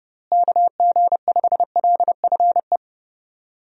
Morse code, heard as a keyed tone, about 30 wpm, 700 Hz. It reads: KG5LFE
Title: USA Call Signs - 6 Characters - Repeated 3x - 30wpm